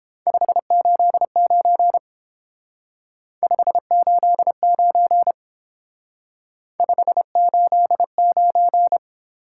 Morse code, heard as a keyed tone, about 28 wpm, 700 Hz. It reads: 589 589 589